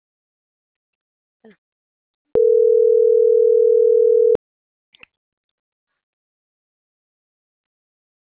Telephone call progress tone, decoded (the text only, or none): ringback tone